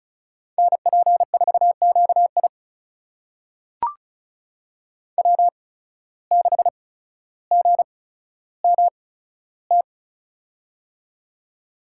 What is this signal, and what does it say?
Morse code, heard as a keyed tone, about 36 wpm, spaced as for 15 wpm, 700 Hz. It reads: NP4QI W6GMT
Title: USA Call Signs - 5 Characters - 8x Character Spacing - 35wpm